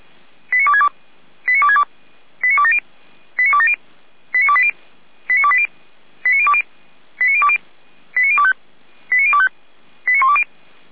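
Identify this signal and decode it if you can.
Five-tone selective call (ZVEI), heard as quick runs of five tones, tones of about 70 ms. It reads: 89262 89262 89270 89270 89280 89280 89020 89020 89025 89025 89120
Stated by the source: ZVEI five-tone selective call, code 89358, received by SDR.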